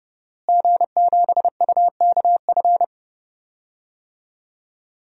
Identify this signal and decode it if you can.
Morse code, heard as a keyed tone, about 30 wpm, 700 Hz. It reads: G7UKF
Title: DXCC-100 Call Signs - 5 Characters - Repeated 2x - 30wpm